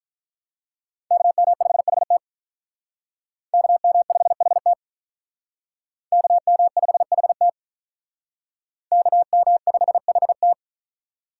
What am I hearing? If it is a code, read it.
Morse code, heard as a keyed tone, about 45 wpm, 700 Hz. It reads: KM5HT KM5HT KM5HT KM5HT